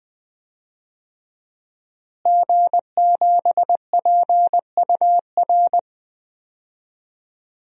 Morse code, heard as a keyed tone, about 20 wpm, 700 Hz. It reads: G7PUR
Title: DXCC-100 Call Signs - 5 Characters - Repeated 3x - 20wpm